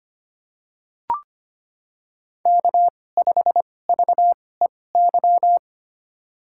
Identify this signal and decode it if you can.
Morse code, heard as a keyed tone, about 25 wpm, spaced as for 18 wpm, 700 Hz. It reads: K5VEY
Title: USA Call Signs - 5 Characters - 2x Character Spacing - 25wpm